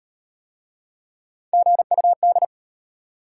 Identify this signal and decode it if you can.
Morse code, heard as a keyed tone, about 35 wpm, 700 Hz. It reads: GUD